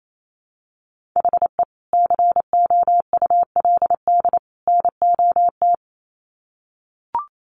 Morse code, heard as a keyed tone, about 28 wpm, 700 Hz. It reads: HE COULD NOT
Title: Sets of 3 Words 28wpm